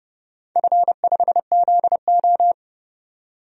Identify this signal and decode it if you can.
Morse code, heard as a keyed tone, about 30 wpm, 700 Hz. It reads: F5ZO
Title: DXCC-100 Call Signs - 4 Characters - 30wpm